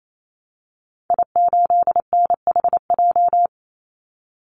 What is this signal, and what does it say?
Morse code, heard as a keyed tone, about 28 wpm, 700 Hz. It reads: I8NHJ
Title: Call Signs - 28wpm